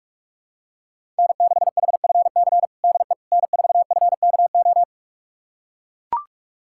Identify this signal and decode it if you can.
Morse code, heard as a keyed tone, about 41 wpm, 700 Hz. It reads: N6HFC DE N4FKO
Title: USA Callsign DE Callsign - 5 Characters - 45wpm